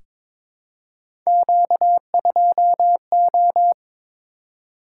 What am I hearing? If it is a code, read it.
Morse code, heard as a keyed tone, about 22 wpm, 700 Hz. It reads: Q2O
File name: Three Letter-Number Combinations - 22wpm